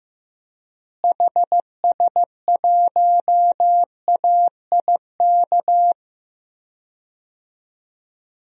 Morse code, heard as a keyed tone, about 15 wpm, 700 Hz. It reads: HS1AIK
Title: DXCC-100 Call Signs - 6 Characters - Repeated 2x - 15wpm